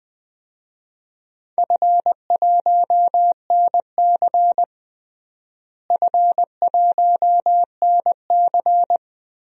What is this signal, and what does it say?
Morse code, heard as a keyed tone, about 20 wpm, 700 Hz. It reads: F1NC F1NC
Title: DXCC-100 Call Signs - 4 Characters - Repeated 2x - 20wpm